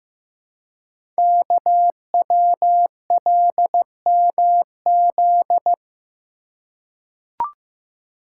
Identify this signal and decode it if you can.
Morse code, heard as a keyed tone, about 15 wpm, 700 Hz. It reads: KWLMZ
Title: Code Groups - 5 Letter:Numbers - 15wpm